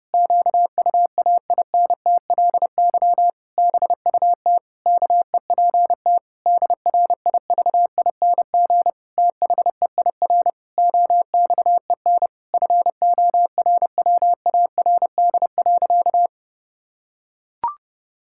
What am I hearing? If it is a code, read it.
Morse code, heard as a keyed tone, about 30 wpm, 700 Hz. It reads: QUAINTLY BUT KEPT DRIVING THEIR OXEN FORWARD.
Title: Pangram Challenge v2 - 30wpm